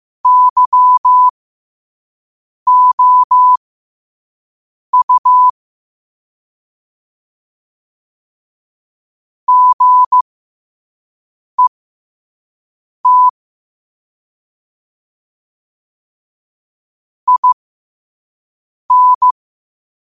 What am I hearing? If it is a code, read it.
Morse code, heard as a keyed tone, about 15 wpm, spaced as for 5 wpm, 1000 Hz. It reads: YOU GET IN
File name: qotdfiles / 5wpm